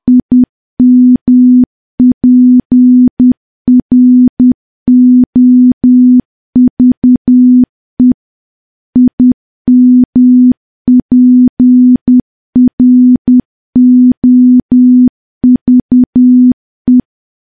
Morse code, heard as a keyed tone, about 10 wpm, 250 Hz. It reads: IMPROVE IMPROVE
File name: morse / morse-training-10WPM